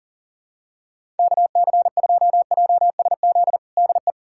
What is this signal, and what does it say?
Morse code, heard as a keyed tone, about 40 wpm, 700 Hz. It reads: KC2JSZ DE
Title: USA Callsign DE Callsign - 6 Characters - 40wpm